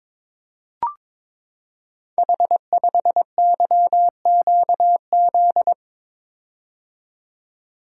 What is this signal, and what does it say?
Morse code, heard as a keyed tone, about 23 wpm, 700 Hz. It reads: H5YQZ
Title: Code Groups - 5 Letter:Numbers - 22wpm